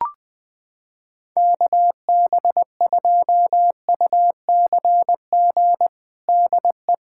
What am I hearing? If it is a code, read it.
Morse code, heard as a keyed tone, about 20 wpm, 700 Hz. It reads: KB2UCG DE
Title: USA Callsign DE Callsign - 6 Characters - 20wpm